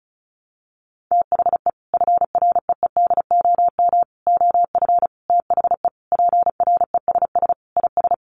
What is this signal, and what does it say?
Morse code, heard as a keyed tone, about 34 wpm, 700 Hz. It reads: THE FREEDOM OF THE PRESS IS
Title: Sentences from Top 800 Words 35wpm